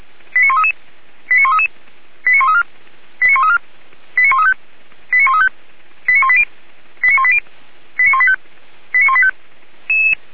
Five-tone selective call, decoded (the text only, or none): 89133 89133 89134 89134 89135 89135 89180 89180 89185 89185